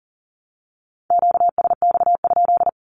Morse code, heard as a keyed tone, about 40 wpm, 700 Hz. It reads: QSX?
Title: Q-Codes 40wpm